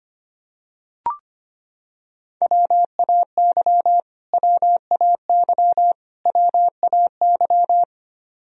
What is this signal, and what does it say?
Morse code, heard as a keyed tone, about 25 wpm, 700 Hz. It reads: WAY WAY WAY
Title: Top 100 Words Repeated 25wpm